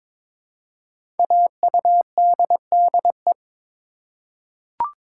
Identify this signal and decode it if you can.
Morse code, heard as a keyed tone, about 22 wpm, 700 Hz. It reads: AUDDE
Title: Code Groups - 5 Letters - 22wpm